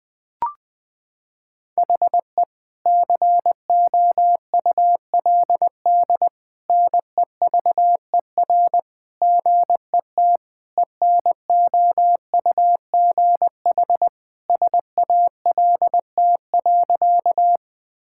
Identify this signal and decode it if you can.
Morse code, heard as a keyed tone, about 19 wpm, 700 Hz. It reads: HE COULD NEVER GET ENOUGH SALT.
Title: Sentences from Top 1000 Words - Review Part 5 - 20wpm